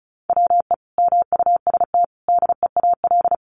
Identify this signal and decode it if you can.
Morse code, heard as a keyed tone, about 34 wpm, 700 Hz. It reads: WE MUST DEAL
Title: Sentences from Top 1400 Words 35wpm